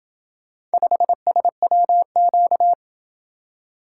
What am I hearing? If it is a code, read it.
Morse code, heard as a keyed tone, about 31 wpm, 700 Hz. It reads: 5SWQ